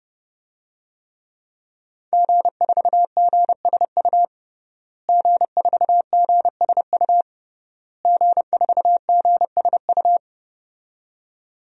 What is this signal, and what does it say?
Morse code, heard as a keyed tone, about 30 wpm, 700 Hz. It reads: G4GSU G4GSU G4GSU
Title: DXCC-100 Call Signs - 5 Characters - Repeated 3x - 30wpm